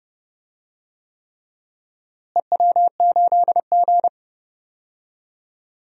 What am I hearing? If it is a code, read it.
Morse code, heard as a keyed tone, about 30 wpm, 700 Hz. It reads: EW8G
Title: DXCC-100 Call Signs - 4 Characters - Repeated 3x - 30wpm